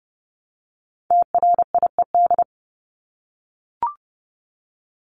Morse code, heard as a keyed tone, about 30 wpm, 700 Hz. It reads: TRIED
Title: ED Suffix - Top 100 Words with Base Words 30wpm